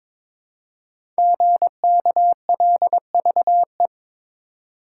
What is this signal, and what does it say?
Morse code, heard as a keyed tone, about 22 wpm, 700 Hz. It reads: GKLVE
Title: Code Groups - 5 Letter:Numbers - 22wpm